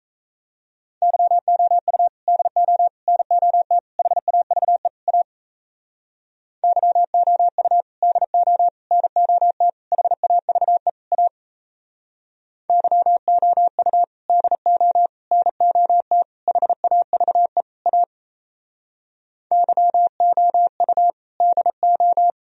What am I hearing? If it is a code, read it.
Morse code, heard as a keyed tone, about 37 wpm, 700 Hz. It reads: YOU DO NOT HAVE A YOU DO NOT HAVE A YOU DO NOT HAVE A YOU DO